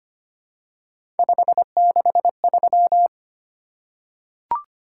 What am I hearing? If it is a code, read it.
Morse code, heard as a keyed tone, about 25 wpm, 700 Hz. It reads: 563
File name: Three Letter-Number Combinations - 25wpm